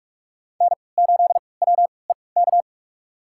Morse code, heard as a keyed tone, about 46 wpm, spaced as for 26 wpm, 700 Hz. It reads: N8WEK
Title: USA Call Signs - 5 Characters - 3x Character Spacing - 45wpm